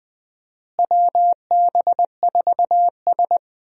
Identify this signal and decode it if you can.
Morse code, heard as a keyed tone, about 20 wpm, 700 Hz. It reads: WB4S
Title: USA Call Signs - 4 Characters - Repeated 2x - 20wpm